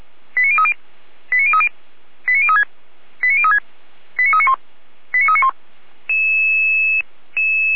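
Five-tone selective call (ZVEI), heard as quick runs of five tones, tones of about 80 ms. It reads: 89030 89030 89036 89036 89391 89391
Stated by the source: ZVEI five-tone selective call received by SDR, from the feuerwehr-orlamuende.org fire brigade archive.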